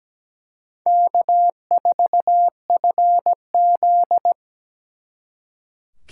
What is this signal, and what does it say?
Morse code, heard as a keyed tone, about 20 wpm, 700 Hz. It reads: K4FZ
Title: Speed Racing - USA 1x2 Call Signs 17wpm